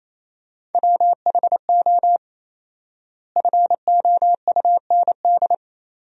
Morse code, heard as a keyed tone, about 28 wpm, 700 Hz. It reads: WHO FOUND
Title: 4x Word-Spacing - Sentences from Top 300 Words - 28wpm